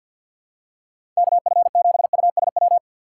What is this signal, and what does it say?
Morse code, heard as a keyed tone, about 51 wpm, 700 Hz. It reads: KF7USW
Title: USA Call Signs - 6 Characters - 4x Character Spacing - 50wpm